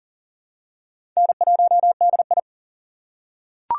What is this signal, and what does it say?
Morse code, heard as a keyed tone, about 40 wpm, 700 Hz. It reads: N1DI
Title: USA Call Signs - 4x Character Spacing - 40wpm